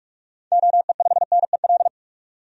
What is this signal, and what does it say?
Morse code, heard as a keyed tone, about 46 wpm, 700 Hz. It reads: OE5NEL
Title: DXCC-100 Call Signs - 6 Characters - 45wpm